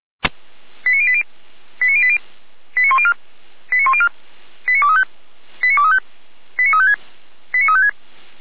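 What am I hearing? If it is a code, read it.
Five-tone selective call (ZVEI), heard as quick runs of five tones, tones of about 70 ms. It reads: 89080 89080 89104 89104 89235 89235 89356 89356
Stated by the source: Received on an SDR; ZVEI five-tone call.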